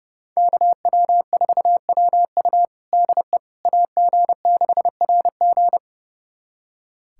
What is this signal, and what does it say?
Morse code, heard as a keyed tone, about 30 wpm, 700 Hz. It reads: KW4WU DE AG6RG
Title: USA Callsign DE Callsign - 5 Characters - 30wpm